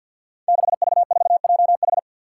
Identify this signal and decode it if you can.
Morse code, heard as a keyed tone, about 50 wpm, 700 Hz. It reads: 6V4JH
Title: Five Challenging Letter-Number Combinations 50wpm